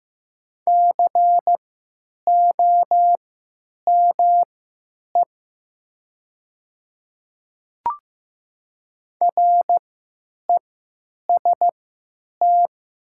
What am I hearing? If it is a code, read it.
Morse code, heard as a keyed tone, about 15 wpm, spaced as for 9 wpm, 700 Hz. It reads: COME REST
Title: 3x Character Spacing - Top 500 Words - 15wpm